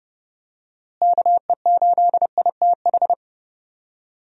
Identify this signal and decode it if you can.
Morse code, heard as a keyed tone, about 31 wpm, 700 Hz. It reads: KE8ITH
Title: USA Call Signs - 6 Characters - Repeated 2x - 30wpm